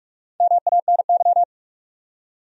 Morse code, heard as a keyed tone, about 45 wpm, 700 Hz. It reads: MANY
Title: Top 100 Words 45wpm